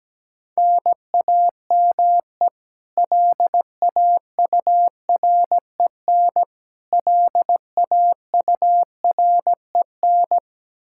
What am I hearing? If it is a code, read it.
Morse code, heard as a keyed tone, about 17 wpm, 700 Hz. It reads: NAME LAUREN LAUREN
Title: Practice QSO - Exchange 1 - Beginning - 17wpm